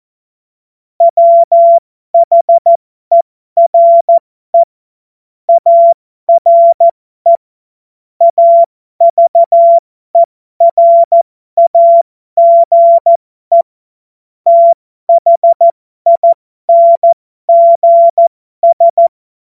Morse code, heard as a keyed tone, about 14 wpm, spaced as for 12 wpm, 680 Hz. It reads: WHERE ARE AVERAGE THINGS
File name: Morsekurs / Witze